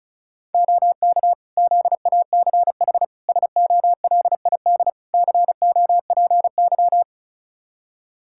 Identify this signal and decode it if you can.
Morse code, heard as a keyed tone, about 34 wpm, 700 Hz. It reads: OK ZACH SOLID COPY
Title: K1USN SST QSO Practice 35wpm